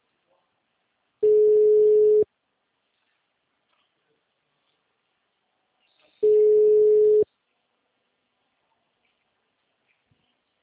Ringback tone (Europe, 425 Hz).